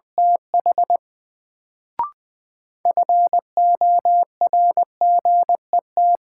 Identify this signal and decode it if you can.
Morse code, heard as a keyed tone, about 20 wpm, 700 Hz. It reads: TH FORGET
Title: Sentences from Top 1200 Words - New Word Drill Alt - 20wpm